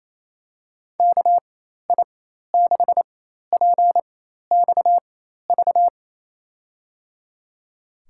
Morse code, heard as a keyed tone, about 27 wpm, spaced as for 13 wpm, 700 Hz. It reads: KI6PXV
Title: USA Call Signs - 6 Characters - 4x Character Spacing - 28wpm